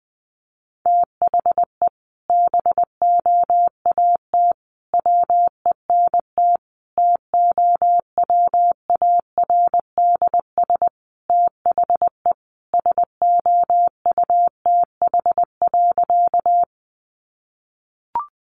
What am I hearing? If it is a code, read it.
Morse code, heard as a keyed tone, about 19 wpm, 700 Hz. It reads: THE BOAT WENT TOWARDS THE SOUTH.
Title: Sentences from Top 1100 Words 20wpm